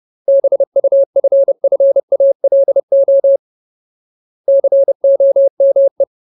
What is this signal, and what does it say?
Morse code, heard as a keyed tone, about 30 wpm, 550 Hz. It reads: BUFFALO COME
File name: Ultimate Word List / 30WPM Ultimate